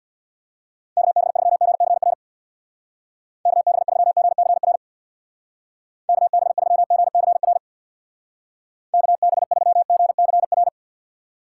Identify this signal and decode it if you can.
Morse code, heard as a keyed tone, about 62 wpm, 700 Hz. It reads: KB3GCR KB3GCR KB3GCR KB3GCR
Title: Speed Racing - USA 2x3 Call Signs 50wpm